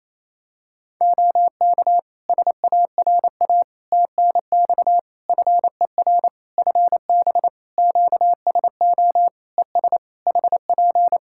Morse code, heard as a keyed tone, about 28 wpm, 700 Hz. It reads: OK SARA TNX FER FB QSO ES HP
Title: Practice QSO - Exchange 4 - Ending - 28wpm